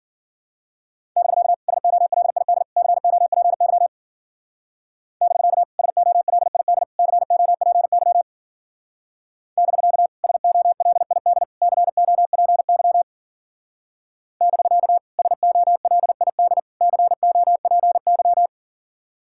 Morse code, heard as a keyed tone, about 52 wpm, 700 Hz. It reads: <BK> SOLID COPY <BK> SOLID COPY <BK> SOLID COPY <BK> SOLID COPY